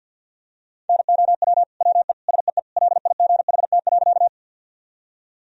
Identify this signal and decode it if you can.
Morse code, heard as a keyed tone, about 52 wpm, 700 Hz. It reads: NOW WE SEE LIGHT.
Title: Sentences from Top 1000 Words - Review Part 1 - 50wpm